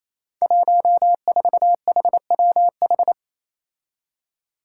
Morse code, heard as a keyed tone, about 28 wpm, 700 Hz. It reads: 14HWH
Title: Five Challenging Letter-Number Combinations 28wpm